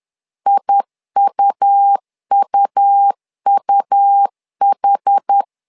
Morse code, heard as a keyed tone, about 10 wpm, 800 Hz. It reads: IUUUH
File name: LEZIONI / 2 UHDN / 2 UHD 10wpm 1826